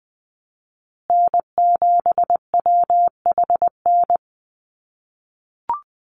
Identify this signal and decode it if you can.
Morse code, heard as a keyed tone, about 20 wpm, 700 Hz. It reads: N7WHN
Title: USA Call Signs Repeated 2x 20wpm